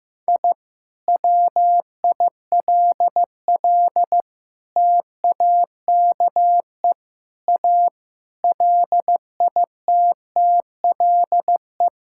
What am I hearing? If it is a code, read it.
Morse code, heard as a keyed tone, about 15 wpm, 700 Hz. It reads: I WILL TAKE A LITTLE
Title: Sentences from Top 1000 Words - Review Part 5 - 15wpm